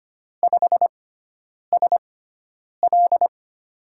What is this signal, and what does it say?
Morse code, heard as a keyed tone, about 25 wpm, spaced as for 9 wpm, 700 Hz. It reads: 5SL
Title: USA Call Signs - 5 Characters - 6x Character Spacing - 25wpm